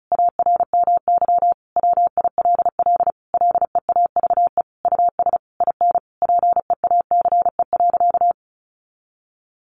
Morse code, heard as a keyed tone, about 34 wpm, 700 Hz. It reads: ARMY WILL LEAVE US IN PEACE.
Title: Sentences from Top 1500 Words 35wpm